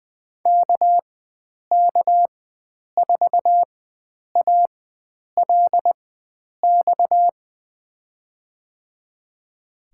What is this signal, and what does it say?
Morse code, heard as a keyed tone, about 20 wpm, spaced as for 9 wpm, 700 Hz. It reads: KK4ALX
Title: USA Call Signs - 6 Characters - 4x Character Spacing - 20wpm